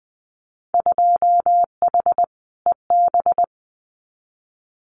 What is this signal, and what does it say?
Morse code, heard as a keyed tone, about 20 wpm, 700 Hz. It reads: 2H EB
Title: Winter Field Day Exchanges Encore 20wpm